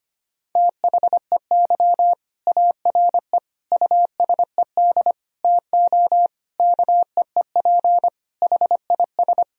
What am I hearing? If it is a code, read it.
Morse code, heard as a keyed tone, about 25 wpm, 700 Hz. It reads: THEY ARE USED TO KEEP HIS
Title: Sentences from Top 1000 Words - Review Part 3 - 25wpm